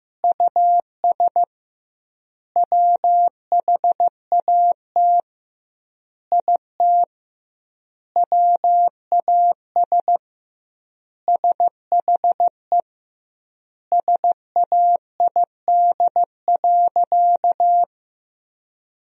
Morse code, heard as a keyed tone, about 15 wpm, 700 Hz. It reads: US WHAT IT WAS SHE SAID.
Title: 2x Word-Spacing - Sentences from Top 400 Words - 15wpm